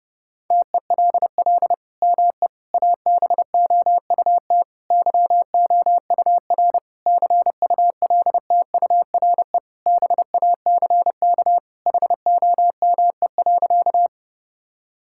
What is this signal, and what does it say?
Morse code, heard as a keyed tone, about 30 wpm, 700 Hz. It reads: TELL ME ABOUT YOUR CULTURE BACK HOME.
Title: Sentences from Top 1000 Words - Review Part 6 - 30wpm